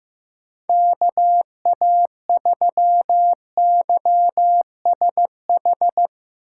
Morse code, heard as a keyed tone, about 15 wpm, 700 Hz. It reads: KA3YSH
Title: USA Call Signs - 6 Characters - 8x Character Spacing - 15wpm